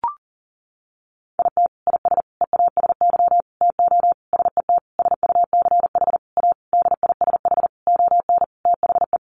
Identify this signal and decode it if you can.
Morse code, heard as a keyed tone, about 40 wpm, 700 Hz. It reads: IT IS EASY TO SET SUCH A DISH ON THE
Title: Sentences from Top 1300 Words 40wpm